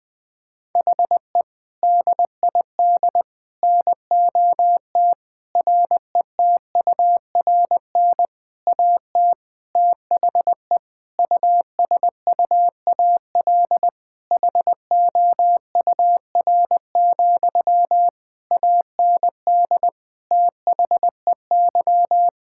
Morse code, heard as a keyed tone, about 21 wpm, 700 Hz. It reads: HE DID NOT RETURN AT THE USUAL HOUR, AND THEY
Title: Sentences from Top 1600 Words 20wpm